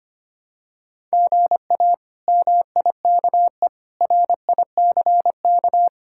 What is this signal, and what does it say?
Morse code, heard as a keyed tone, about 25 wpm, 700 Hz. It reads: GA MIKE RICK
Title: K1USN SST QSO Practice - 25wpm